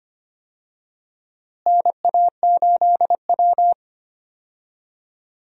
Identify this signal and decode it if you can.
Morse code, heard as a keyed tone, about 25 wpm, 700 Hz. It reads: NA8W
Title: USA Call Signs - 4 Characters - Repeated 3x - 25wpm